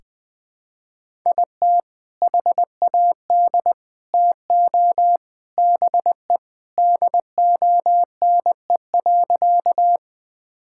Morse code, heard as a keyed tone, about 20 wpm, 700 Hz. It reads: IT HAD TO BE DONE.
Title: Sentences with 3-Word Phrases - 20wpm